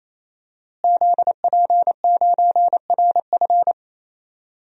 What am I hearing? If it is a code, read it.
Morse code, heard as a keyed tone, about 28 wpm, 700 Hz. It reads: ZP9RF
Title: DXCC-100 Call Signs - 5 Characters - Repeated 3x - 28wpm